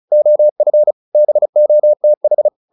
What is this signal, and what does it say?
Morse code, heard as a keyed tone, about 34 wpm, 600 Hz. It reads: OF BOTH
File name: the old man and the sea 2 35wpm